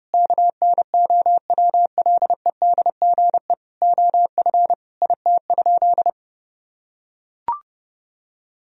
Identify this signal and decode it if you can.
Morse code, heard as a keyed tone, about 30 wpm, 700 Hz. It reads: KNOWLEDGE OF IT?
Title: Sentences from Top 1000 Words - Review Part 3 - 30wpm